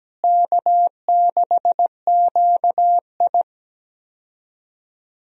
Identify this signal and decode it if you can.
Morse code, heard as a keyed tone, about 17 wpm, 700 Hz. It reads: K6QI
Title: USA Call Signs - 4 Characters - Repeated 3x - 17wpm